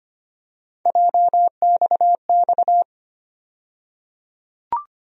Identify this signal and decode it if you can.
Morse code, heard as a keyed tone, about 25 wpm, 700 Hz. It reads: JXX
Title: USA Call Sign Postfixes - 3 Letters 25wpm